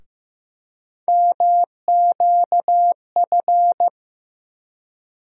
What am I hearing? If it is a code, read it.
Morse code, heard as a keyed tone, about 16 wpm, 700 Hz. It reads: MQF
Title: USA Call Sign Postfixes - 3 Letters 15wpm